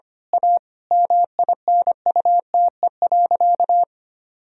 Morse code, heard as a keyed tone, about 25 wpm, 700 Hz. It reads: A MINUTE.
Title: Sentences from Top 1000 Words - Review Part 1 - 25wpm